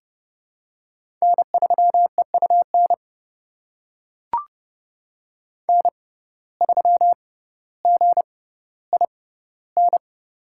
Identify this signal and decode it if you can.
Morse code, heard as a keyed tone, about 30 wpm, spaced as for 15 wpm, 700 Hz. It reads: N3EUN N3GIN
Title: USA Call Signs - 5 Characters - 6x Character Spacing - 30wpm